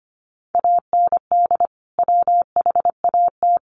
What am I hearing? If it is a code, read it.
Morse code, heard as a keyed tone, about 25 wpm, 700 Hz. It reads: AND WHAT
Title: Sentences from Top 1000 Words 25wpm